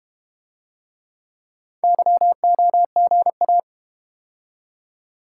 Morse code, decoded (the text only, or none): YOGA